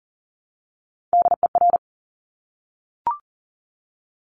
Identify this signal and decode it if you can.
Morse code, heard as a keyed tone, about 42 wpm, 700 Hz. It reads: DER
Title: Top Three Letter Combinations in Words 40wpm